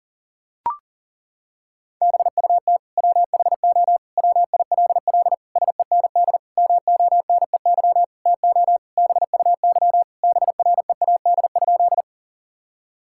Morse code, heard as a keyed tone, about 40 wpm, 700 Hz. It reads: BUT WHO WILL SEND MONEY TO BUY BREAD?
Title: Sentences from Top 1600 Words 40wpm